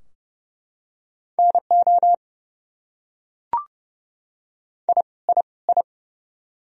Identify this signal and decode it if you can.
Morse code, heard as a keyed tone, about 30 wpm, 700 Hz. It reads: NO I I I
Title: Top 100 Words Repeated 30wpm